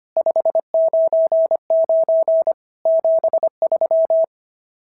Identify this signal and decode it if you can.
Morse code, heard as a keyed tone, about 25 wpm, 620 Hz. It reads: 599 73
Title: SOTA QSO Practice 25wpm